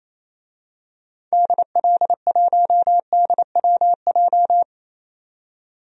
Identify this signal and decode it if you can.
Morse code, heard as a keyed tone, about 27 wpm, 700 Hz. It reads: DL1DWJ
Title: DXCC-100 Call Signs - 6 Characters - 28wpm